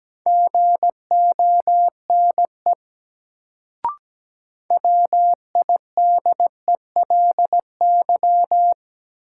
Morse code, heard as a keyed tone, about 17 wpm, 700 Hz. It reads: GONE WIDELY
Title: Sentences from Top 1200 Words - New Word Drill - 17wpm